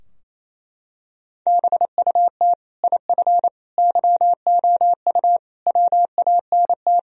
Morse code, heard as a keyed tone, about 28 wpm, 700 Hz. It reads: BUT IF YOU WANT
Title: Sentences from Top 1000 Words - Review Part 3 - 28wpm